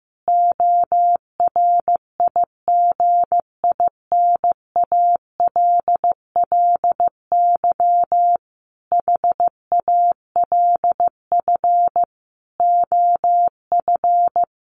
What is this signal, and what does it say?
Morse code, heard as a keyed tone, about 14 wpm, 700 Hz. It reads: ORIGINALLY HALF OF
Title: Sentences from Top 1500 Words 15wpm